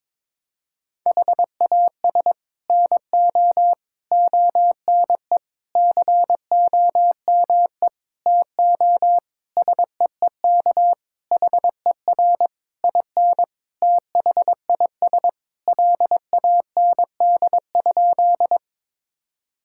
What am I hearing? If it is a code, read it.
Morse code, heard as a keyed tone, about 21 wpm, 700 Hz. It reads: HAS NO ONE COME TO SEEK HER IN THIS LAND?
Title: Sentences from Top 1600 Words 22wpm